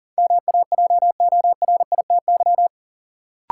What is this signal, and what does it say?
Morse code, heard as a keyed tone, about 40 wpm, 700 Hz. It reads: MAJORITY
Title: Sets of 3 Words - New Word Drill - 40wpm